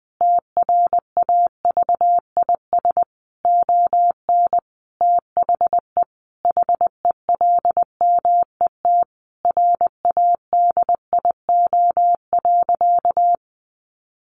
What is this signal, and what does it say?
Morse code, heard as a keyed tone, about 20 wpm, 700 Hz. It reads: TRAVIS ON THE HELMET RADIO.
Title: A Sound Of Thunder 20wpm